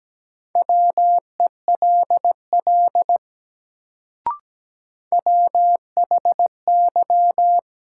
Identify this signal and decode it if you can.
Morse code, heard as a keyed tone, about 17 wpm, 700 Hz. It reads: WELL WHY
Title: Sentences from Top 200 Words - New Word Drill - 17wpm